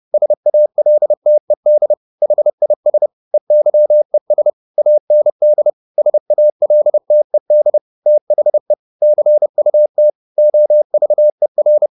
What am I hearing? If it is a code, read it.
Morse code, heard as a keyed tone, about 30 wpm, 600 Hz. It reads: SALTED HIS EYES AND SALTED THE CUT OVER